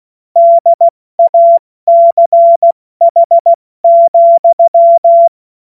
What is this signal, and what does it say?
Morse code, heard as a keyed tone, about 16 wpm, 680 Hz. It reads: DACH,
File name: Morsekurs / Witze / Witz 07. September 2021, 70 F80